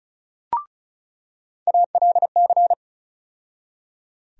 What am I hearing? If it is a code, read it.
Morse code, heard as a keyed tone, about 36 wpm, 700 Hz. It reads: ALC
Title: USA Call Sign Postfixes - 3 Letters 35wpm